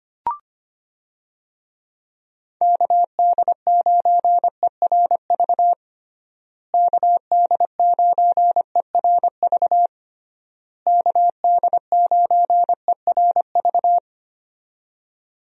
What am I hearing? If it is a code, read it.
Morse code, heard as a keyed tone, about 25 wpm, 700 Hz. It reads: KD9ERV KD9ERV KD9ERV
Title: USA Call Signs - 6 Characters - Repeated 3x - 25wpm